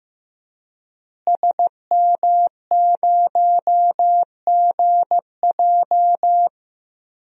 Morse code, heard as a keyed tone, about 15 wpm, 700 Hz. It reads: SM0GJ
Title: DXCC-100 Call Signs - 5 Characters - Repeated 3x - 15wpm